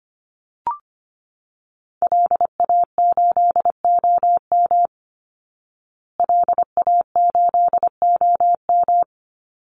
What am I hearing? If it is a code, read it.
Morse code, heard as a keyed tone, about 25 wpm, 700 Hz. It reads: LA8OM LA8OM
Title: Call Signs Repeated 3x - 25wpm